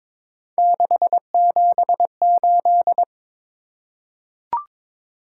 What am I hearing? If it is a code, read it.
Morse code, heard as a keyed tone, about 22 wpm, 700 Hz. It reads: 678
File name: practice / 22wpm / Three Letter-Number Combinations - 22wpm